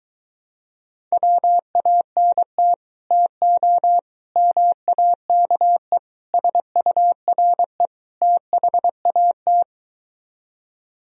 Morse code, heard as a keyed tone, about 22 wpm, 700 Hz. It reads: WANT TO MAKE SURE THAT